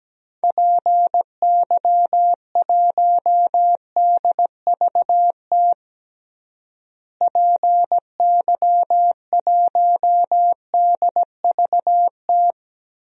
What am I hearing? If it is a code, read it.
Morse code, heard as a keyed tone, about 17 wpm, 700 Hz. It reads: PY1DVT PY1DVT
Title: DXCC-100 Call Signs - 6 Characters - Repeated 3x - 17wpm